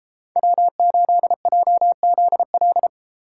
Morse code, heard as a keyed tone, about 32 wpm, 700 Hz. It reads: W8JZL